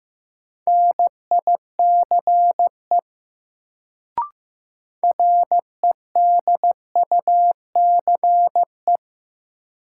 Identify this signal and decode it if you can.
Morse code, heard as a keyed tone, about 16 wpm, 700 Hz. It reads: NICE REDUCE
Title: Sentences from Top 1100 Words - New Word Drill - 15wpm